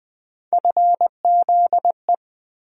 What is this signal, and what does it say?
Morse code, heard as a keyed tone, about 20 wpm, 700 Hz. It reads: FZE
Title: USA Call Sign Postfixes - 3 Letters 20wpm